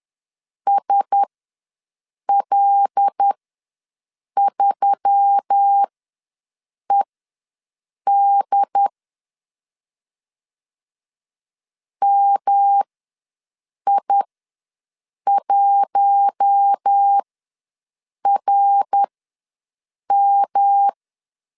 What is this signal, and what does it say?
Morse code, heard as a keyed tone, about 10 wpm, spaced as for 6 wpm, 800 Hz. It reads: SL3ED MI1RM
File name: LEZIONI / 5 LBVC / 5 L ws 10wpm 3636